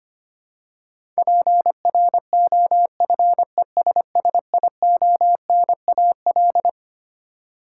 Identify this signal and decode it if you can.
Morse code, heard as a keyed tone, about 25 wpm, 700 Hz. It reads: PROFESSIONAL
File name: Twelve Letter Words - 25wpm